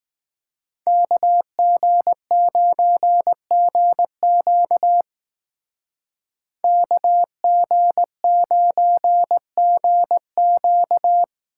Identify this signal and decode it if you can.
Morse code, heard as a keyed tone, about 18 wpm, 700 Hz. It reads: KG9GQ KG9GQ
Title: Speed Racing - USA 2x2 Call Signs 15wpm